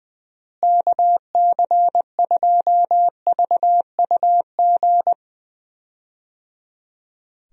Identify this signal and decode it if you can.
Morse code, heard as a keyed tone, about 20 wpm, 700 Hz. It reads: KC2VUG
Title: USA Call Signs - 6 Characters - Repeated 3x - 20wpm